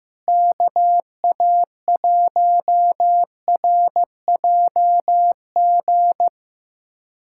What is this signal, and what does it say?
Morse code, heard as a keyed tone, about 15 wpm, 700 Hz. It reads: KA1RJG
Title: USA Callsign DE Callsign 15wpm